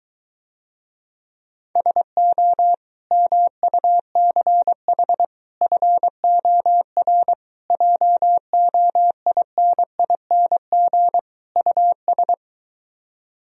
Morse code, heard as a keyed tone, about 22 wpm, 700 Hz. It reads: SO MUCH FOR JOINING US